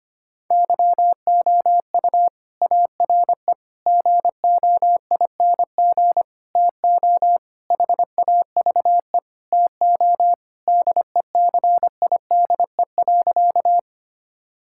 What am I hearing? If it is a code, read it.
Morse code, heard as a keyed tone, about 25 wpm, 700 Hz. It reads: YOU ARE GOING TO HAVE TO DECIDE.
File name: practice / 25wpm / Sentences with 4-Word Phrases - 25wpm